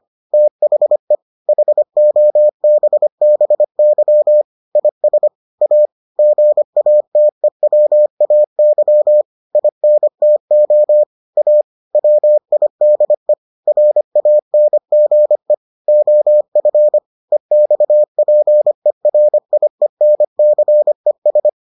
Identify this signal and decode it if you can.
Morse code, heard as a keyed tone, about 25 wpm, 600 Hz. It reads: THE HOBBY IS A GATEWAY INTO A WIDE RANGE OF EXPERIENCES